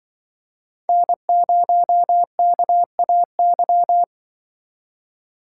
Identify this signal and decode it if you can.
Morse code, heard as a keyed tone, about 24 wpm, 700 Hz. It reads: N0KAY
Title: Speed Racing - USA 1x3 Call Signs 20wpm